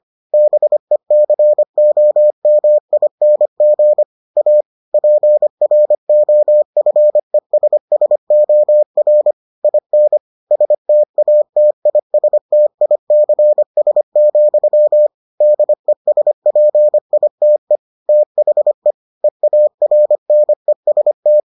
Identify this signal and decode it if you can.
Morse code, heard as a keyed tone, about 25 wpm, 600 Hz. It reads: BECOMING A PROFESSOR IN STATISTICS, DESPITE THE EARNEST